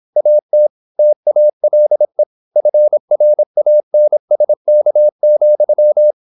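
Morse code, heard as a keyed tone, about 26 wpm, 600 Hz. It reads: AT TALE FRANSK,
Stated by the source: High-speed (QRQ) Morse at 26 wpm.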